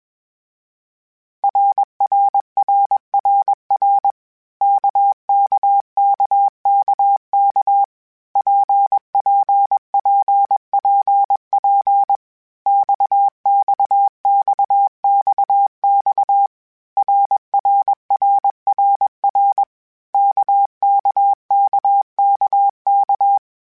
Morse code, heard as a keyed tone, about 21 wpm, 800 Hz. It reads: RRRRR KKKKK PPPPP XXXXX RRRRR KKKKK
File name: Groups@20 / Groups@20 mp3s / RKPX@20